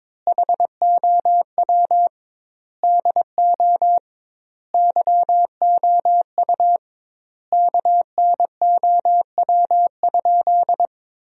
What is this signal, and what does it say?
Morse code, heard as a keyed tone, about 22 wpm, 700 Hz. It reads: HOW DO YOU KNOW?
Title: 2x Word-Spacing - Sentences from Top 100 Words - 22wpm